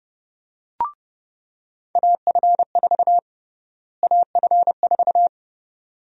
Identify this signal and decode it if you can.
Morse code, heard as a keyed tone, about 30 wpm, 700 Hz. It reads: AF4 AF4
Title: USA Call Sign Prefixes - Repeated 2x - 30wpm